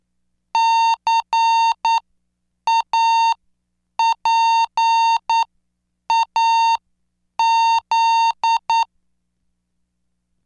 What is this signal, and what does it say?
Morse code, heard as a keyed tone, about 9 wpm, spaced as for 7 wpm, 910 Hz. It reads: CAPAZ